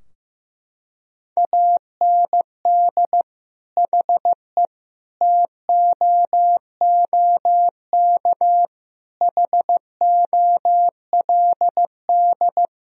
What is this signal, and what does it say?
Morse code, heard as a keyed tone, about 15 wpm, 700 Hz. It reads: AND HE TOOK HOLD
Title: Sentences from Top 1000 Words - Review Part 4 - 15wpm